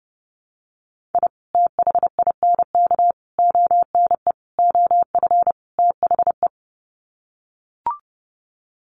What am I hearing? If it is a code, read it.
Morse code, heard as a keyed tone, about 30 wpm, 700 Hz. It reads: I THINK ONE OF THE
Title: Sets of 5 Words 30wpm